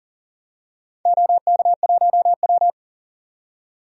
Morse code, heard as a keyed tone, about 38 wpm, 700 Hz. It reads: OK1W
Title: DXCC-100 Call Signs - 4 Characters - 40wpm